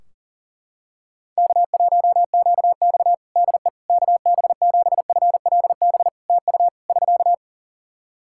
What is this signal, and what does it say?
Morse code, heard as a keyed tone, about 37 wpm, 700 Hz. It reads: K1QX DE KB7FLB TU <SK>
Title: Practice QSO - Exchange 4 - Ending - 40wpm